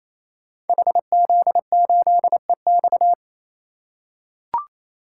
Morse code, heard as a keyed tone, about 27 wpm, 700 Hz. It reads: HZ8EX
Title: Code Groups - 5 Letter:Numbers - 28wpm